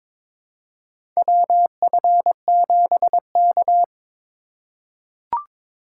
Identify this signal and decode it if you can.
Morse code, heard as a keyed tone, about 23 wpm, 700 Hz. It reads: WF7K
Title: USA Call Signs - 4 Characters - 6x Character Spacing - 22wpm